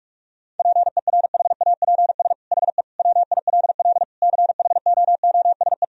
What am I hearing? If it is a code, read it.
Morse code, heard as a keyed tone, about 46 wpm, 700 Hz. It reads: PERHAPS HE WILL CHOOSE
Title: Sentences from Top 1000 Words - Review Part 4 - 45wpm